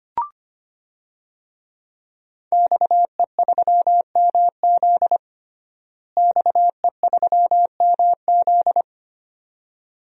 Morse code, heard as a keyed tone, about 25 wpm, 700 Hz. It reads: XE3MZ XE3MZ
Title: DXCC-100 Call Signs - 5 Characters - Repeated 2x - 25wpm